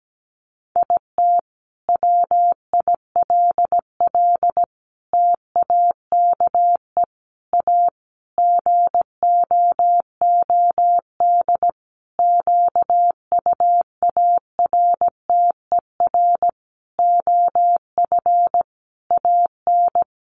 Morse code, heard as a keyed tone, about 17 wpm, 700 Hz. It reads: IT WILL TAKE A GOOD QUARTER OF AN
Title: Sentences from Top 1300 Words 17wpm